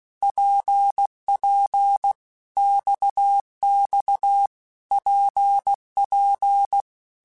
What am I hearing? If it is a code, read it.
Morse code, heard as a keyed tone, about 16 wpm, 800 Hz. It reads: PP XX PP
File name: Basic Letter Groups By7Groups / RKPX / RKPX@15